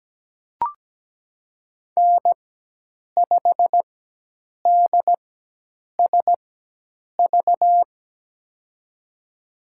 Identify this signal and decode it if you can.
Morse code, heard as a keyed tone, about 17 wpm, spaced as for 8 wpm, 700 Hz. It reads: N5DSV
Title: USA Call Signs - 5 Characters - 4x Character Spacing - 17wpm